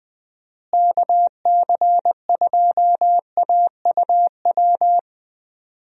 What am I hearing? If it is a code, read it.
Morse code, heard as a keyed tone, about 21 wpm, 700 Hz. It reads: KC2AUW
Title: USA Call Signs - 6 Characters - Repeated 2x - 20wpm